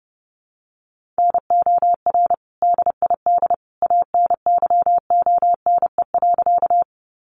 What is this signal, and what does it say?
Morse code, heard as a keyed tone, about 30 wpm, 700 Hz. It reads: NOR DID ANYONE.
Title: Sentences from Top 1500 Words 30wpm